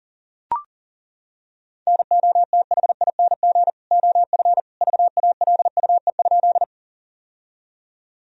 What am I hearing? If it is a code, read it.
Morse code, heard as a keyed tone, about 40 wpm, 700 Hz. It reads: NOTHING OF VALUE?
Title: Sentences from Top 1000 Words - Review Part 3 - 40wpm